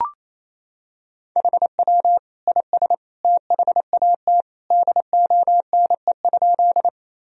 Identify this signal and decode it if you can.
Morse code, heard as a keyed tone, about 28 wpm, 700 Hz. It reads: HW IS THAT DONE?
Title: Sentences from Top 500 Words and Abbreviations 28wpm